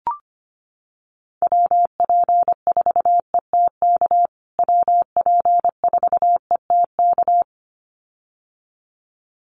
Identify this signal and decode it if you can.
Morse code, heard as a keyed tone, about 25 wpm, 700 Hz. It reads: WP4ETK WP4ETK
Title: USA Call Signs Repeated 2x 25wpm